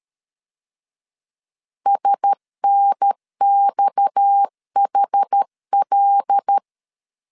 Morse code, heard as a keyed tone, about 13 wpm, 800 Hz. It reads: SNXHL
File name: CW / DET / 1 6 1500C S C 12wpm 3533